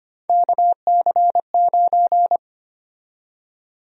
Morse code, heard as a keyed tone, about 25 wpm, 700 Hz. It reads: KC9
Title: USA Call Sign Prefixes - 25wpm